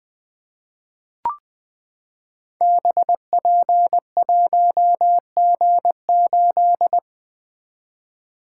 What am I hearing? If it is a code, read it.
Morse code, heard as a keyed tone, about 20 wpm, 700 Hz. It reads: BP1G8
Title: Code Groups - 5 Letter:Numbers - 20wpm